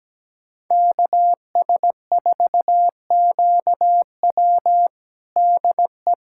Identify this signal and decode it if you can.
Morse code, heard as a keyed tone, about 17 wpm, 700 Hz. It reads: KS4QW DE
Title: USA Callsign DE Callsign - 5 Characters - 17wpm